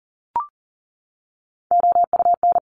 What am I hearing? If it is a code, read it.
Morse code, heard as a keyed tone, about 40 wpm, 700 Hz. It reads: OUN
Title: Top Three Letter Combinations in Words 40wpm